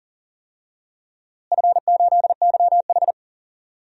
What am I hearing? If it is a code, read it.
Morse code, heard as a keyed tone, about 37 wpm, 700 Hz. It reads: F8YH
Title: DXCC-100 Call Signs - 4 Characters - Repeated 2x - 40wpm